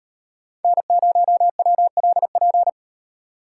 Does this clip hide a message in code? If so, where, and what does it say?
Yes: Morse code from 0.6 to 2.7 s, heard as a keyed tone, about 36 wpm, 700 Hz. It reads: N0WLP